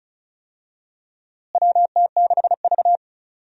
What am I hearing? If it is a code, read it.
Morse code, heard as a keyed tone, about 36 wpm, 700 Hz. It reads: WT6V
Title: USA Call Signs - 4 Characters - Repeated 3x - 35wpm